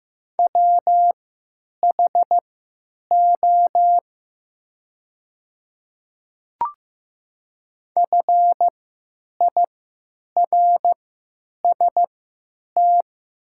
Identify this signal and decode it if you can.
Morse code, heard as a keyed tone, about 15 wpm, spaced as for 9 wpm, 700 Hz. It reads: WHO FIRST
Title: 3x Character Spacing - Top 100 Words - 15wpm